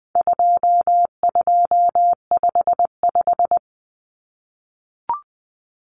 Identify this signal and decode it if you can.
Morse code, heard as a keyed tone, about 20 wpm, 700 Hz. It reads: 2255
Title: Numbers - 4-Digit 20wpm